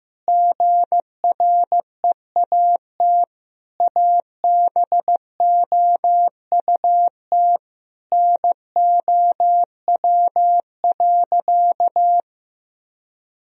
Morse code, heard as a keyed tone, about 15 wpm, 700 Hz. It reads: GREAT ABOUT NOW.
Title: Sentences from Top 1000 Words - Review Part 1 - 15wpm